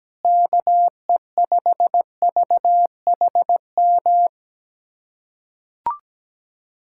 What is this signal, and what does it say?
Morse code, heard as a keyed tone, about 17 wpm, 700 Hz. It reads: KE5VHM
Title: USA Callsign DE Callsign 17wpm